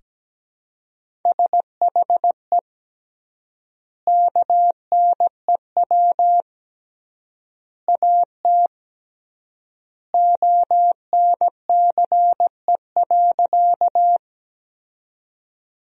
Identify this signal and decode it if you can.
Morse code, heard as a keyed tone, about 17 wpm, 700 Hz. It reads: SHE KNEW AT ONCE.
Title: 3x Word-Spacing - Sentences from Top 400 Words - 17wpm